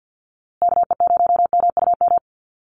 Morse code, heard as a keyed tone, about 50 wpm, 700 Hz. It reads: KE0MUM
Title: USA Call Signs Repeated 3x 50wpm